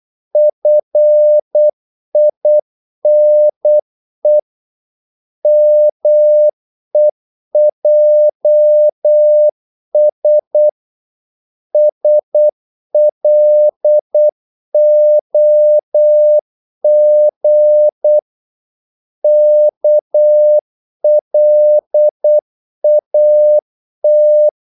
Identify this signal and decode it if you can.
Morse code, heard as a keyed tone, about 8 wpm, 600 Hz. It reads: FINE MEJS SLOG KLAT
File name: Korte ord DK 8wpm